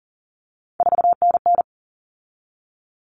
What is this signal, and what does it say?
Morse code, heard as a keyed tone, about 40 wpm, 700 Hz. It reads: 4NN
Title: Common Cut Numbers - 3-Digit 40wpm